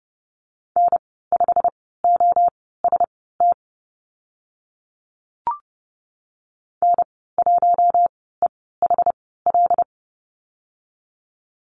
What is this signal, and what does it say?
Morse code, heard as a keyed tone, about 30 wpm, spaced as for 17 wpm, 700 Hz. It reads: N5OST N1EHL
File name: Lesson 15g - Number 5 - Call Signs 30wpm